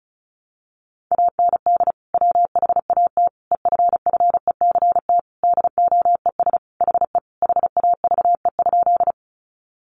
Morse code, heard as a keyed tone, about 34 wpm, 700 Hz. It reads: AND WHAT EFFECT DOES HE HAVE?
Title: Sentences from Top 1000 Words 35wpm